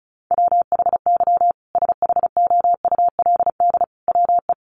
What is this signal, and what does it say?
Morse code, heard as a keyed tone, about 34 wpm, 700 Hz. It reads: WHY SHOULD WE
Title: Sentences from Top 1500 Words 35wpm